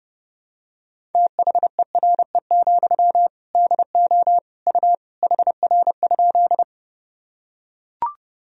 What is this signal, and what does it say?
Morse code, heard as a keyed tone, about 30 wpm, 700 Hz. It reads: THERE, DO U HR?
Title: Sentences from Top 300 Words and Abbreviations 30wpm